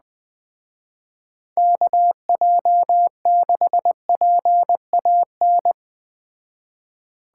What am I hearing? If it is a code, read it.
Morse code, heard as a keyed tone, about 20 wpm, 700 Hz. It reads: KJ6PAN
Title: Speed Racing - USA 2x3 Call Signs 17wpm